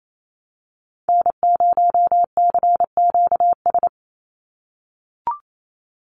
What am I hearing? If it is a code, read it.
Morse code, heard as a keyed tone, about 28 wpm, 700 Hz. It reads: N0CQS
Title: USA Call Signs 28wpm